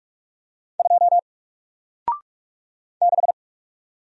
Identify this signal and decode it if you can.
Morse code, heard as a keyed tone, about 44 wpm, 700 Hz. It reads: 2 6
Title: Single Letter-Number Alt 45wpm